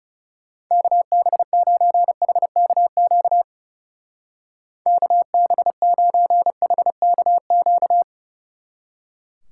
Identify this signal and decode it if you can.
Morse code, heard as a keyed tone, about 35 wpm, 700 Hz. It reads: KB9HKQ KB9HKQ
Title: Speed Racing - USA 2x3 Call Signs 30wpm